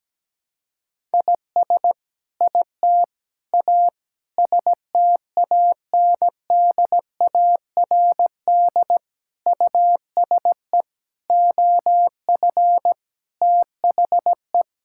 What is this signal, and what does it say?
Morse code, heard as a keyed tone, about 17 wpm, 700 Hz. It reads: IS IT A STANDARD USE OF THE
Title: Sentences from Top 1000 Words - Review Part 6 - 17wpm